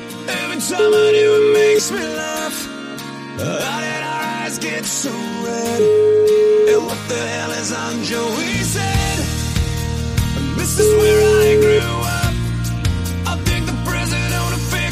Ringback tone (Europe, 425 Hz), with noise.